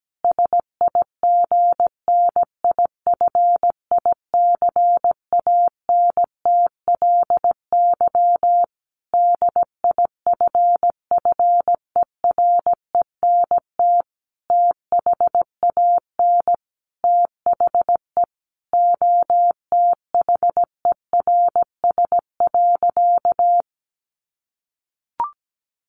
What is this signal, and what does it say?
Morse code, heard as a keyed tone, about 17 wpm, 700 Hz. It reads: SIGNIFICANTLY DIFFERENT THAN THE OTHERS.
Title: Sentences from Top 1500 Words 17wpm